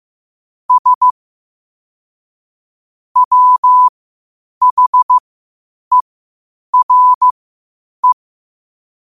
Morse code, heard as a keyed tone, about 15 wpm, spaced as for 8 wpm, 1000 Hz. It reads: S WHERE
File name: qotdfiles / 8wpm